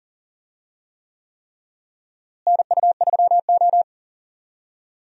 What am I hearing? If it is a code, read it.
Morse code, heard as a keyed tone, about 40 wpm, 700 Hz. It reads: NU3O
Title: USA Call Signs - 4 Characters - Repeated 3x - 40wpm